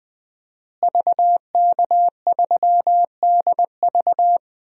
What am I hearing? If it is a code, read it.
Morse code, heard as a keyed tone, about 20 wpm, 700 Hz. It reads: VK3DV
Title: DXCC-100 Call Signs - 5 Characters - Repeated 2x - 20wpm